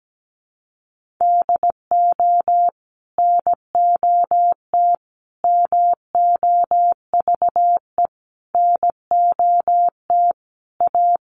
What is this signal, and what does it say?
Morse code, heard as a keyed tone, about 17 wpm, 700 Hz. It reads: DO NOT MOVE NOT A
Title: Sentences from Top 1300 Words 17wpm